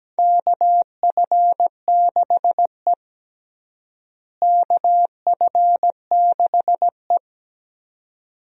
Morse code, heard as a keyed tone, about 17 wpm, 700 Hz. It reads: KF6E KF6E
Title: USA Call Signs - 4 Characters - Repeated 2x - 17wpm